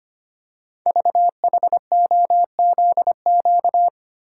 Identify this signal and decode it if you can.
Morse code, heard as a keyed tone, about 25 wpm, 700 Hz. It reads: VHOZQ
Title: Code Groups - 5 Letters - 25wpm